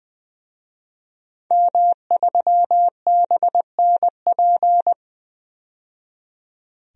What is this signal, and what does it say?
Morse code, heard as a keyed tone, about 20 wpm, 700 Hz. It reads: M3BNP
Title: DXCC-100 Call Signs - 5 Characters - 20wpm